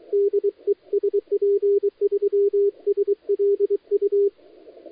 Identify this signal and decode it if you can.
Morse code, heard as a keyed tone, about 23 wpm, 400 Hz. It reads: DE SP3SLU
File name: dlazaawansowanych / nagraniazpasma / titawaNETcontrol